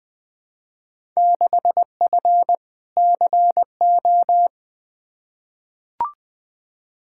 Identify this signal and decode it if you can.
Morse code, heard as a keyed tone, about 20 wpm, 700 Hz. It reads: 6F CO
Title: Field Day Exchanges Encore - 20wpm